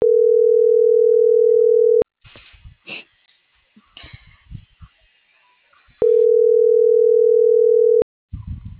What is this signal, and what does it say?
Ringback tone (North America), 440 + 480 Hz.
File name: CENTRAL NMP / out brayan.gutierreznmp